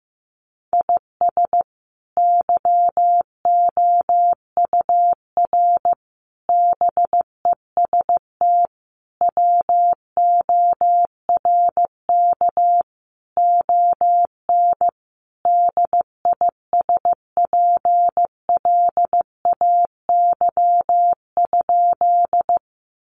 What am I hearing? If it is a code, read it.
Morse code, heard as a keyed tone, about 15 wpm, 700 Hz. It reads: IS YOUR BEST WORK ON DISPLAY?
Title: Sentences from Top 1200 Words 15wpm